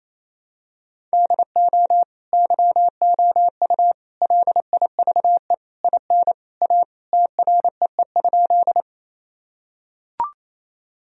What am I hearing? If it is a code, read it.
Morse code, heard as a keyed tone, about 28 wpm, 700 Hz. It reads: DO YOU LIVE IN A TREE?
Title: Sentences from Top 1000 Words - Review Part 1 - 28wpm